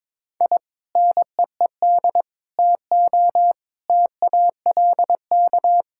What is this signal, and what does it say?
Morse code, heard as a keyed tone, about 21 wpm, 700 Hz. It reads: I NEED TO TALK
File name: practice / 22wpm / Sentences with 4-Word Phrases - 22wpm